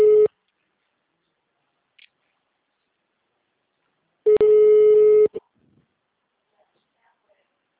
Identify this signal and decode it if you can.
Ringback tone (Europe, 425 Hz).